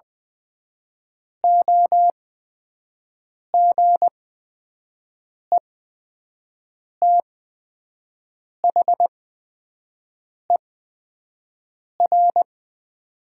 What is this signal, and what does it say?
Morse code, heard as a keyed tone, about 20 wpm, spaced as for 5 wpm, 700 Hz. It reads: OGETHER
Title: 8x Character Spacing - Top 500 Words - 20wpm